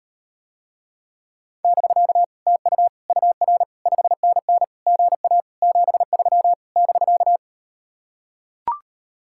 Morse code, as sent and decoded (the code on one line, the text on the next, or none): -...-.- / - ..- / ..- .-. / ..... -. -. / --. .- / --... ...-- / -...-.-
<BK> TU UR 5NN GA 73 <BK>